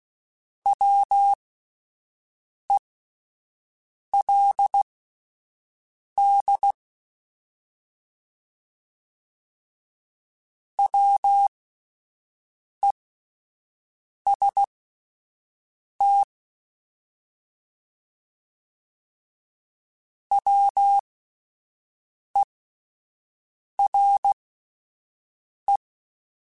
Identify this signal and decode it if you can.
Morse code, heard as a keyed tone, about 16 wpm, spaced as for 5 wpm, 800 Hz. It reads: WELD WEST WERE
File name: Alphabetical / Alphabetical mp3s / Ws @5